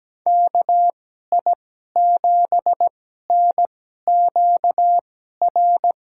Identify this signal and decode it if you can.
Morse code, heard as a keyed tone, about 17 wpm, spaced as for 12 wpm, 700 Hz. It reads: KI7NQR
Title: USA Call Signs - 6 Characters - 2x Character Spacing - 17wpm